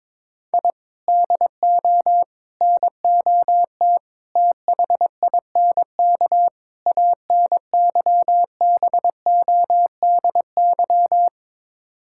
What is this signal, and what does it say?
Morse code, heard as a keyed tone, about 21 wpm, 700 Hz. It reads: I DO NOT THINK ANYBODY
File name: practice / 22wpm / Build A Phrase - 5-Word Phrases - 22wpm